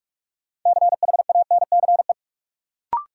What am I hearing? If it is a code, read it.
Morse code, heard as a keyed tone, about 43 wpm, 700 Hz. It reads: CHANCE
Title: Sentences from Top 800 Words - New Word Drill - 45wpm